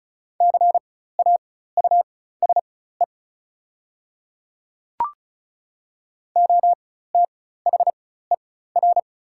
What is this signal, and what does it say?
Morse code, heard as a keyed tone, about 36 wpm, spaced as for 16 wpm, 700 Hz. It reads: CAUSE OTHER
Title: 4x Character Spacing - Top 200 Words - 35wpm